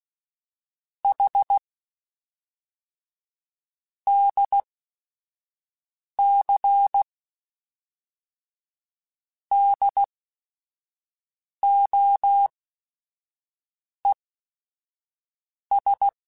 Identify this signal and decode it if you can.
Morse code, heard as a keyed tone, about 16 wpm, spaced as for 5 wpm, 800 Hz. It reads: H DC DOES
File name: Technical Subjects / Technical Subjects-mp3s / Sw Pwr Sup @5